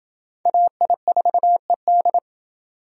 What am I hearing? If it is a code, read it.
Morse code, heard as a keyed tone, about 28 wpm, 700 Hz. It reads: AI4ED